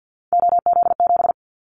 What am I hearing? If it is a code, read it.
Morse code, heard as a keyed tone, about 50 wpm, 700 Hz. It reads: OZ7
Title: Callsign Prefixes for Top 100 DXCC - 50wpm